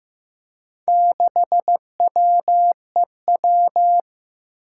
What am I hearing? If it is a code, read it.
Morse code, heard as a keyed tone, about 15 wpm, 700 Hz. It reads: 6WEW
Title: Four Challenging Letter-Number Combinations 15wpm